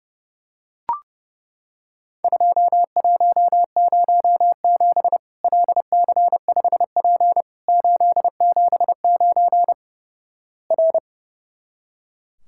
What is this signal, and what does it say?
Morse code, heard as a keyed tone, about 32 wpm, 700 Hz. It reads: 2107 LC5P 879 R
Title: Worked All Europe DX Contest - QTC Exchange - 30wpm